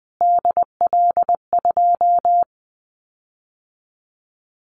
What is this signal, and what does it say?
Morse code, heard as a keyed tone, about 19 wpm, 700 Hz. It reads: DL2
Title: Callsign Prefixes for Top 25 DXCC - Repeated 3x - 20wpm